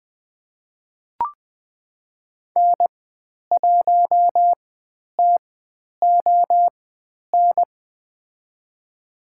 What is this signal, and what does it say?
Morse code, heard as a keyed tone, about 20 wpm, spaced as for 10 wpm, 700 Hz. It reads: N1TON
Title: 095 Lesson 16g - Letter c - Call Signs 20wpm